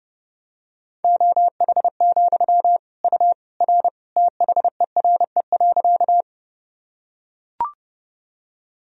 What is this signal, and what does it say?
Morse code, heard as a keyed tone, about 30 wpm, 700 Hz. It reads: OH, U R THERE.
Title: Sentences from Top 500 Words and Abbreviations 30wpm